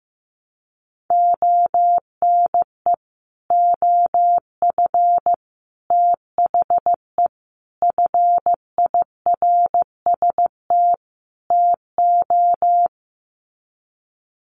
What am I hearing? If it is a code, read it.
Morse code, heard as a keyed tone, about 15 wpm, 700 Hz. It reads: ONE OF THE FIRST TO
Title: Sets of 5 Words Encore 15wpm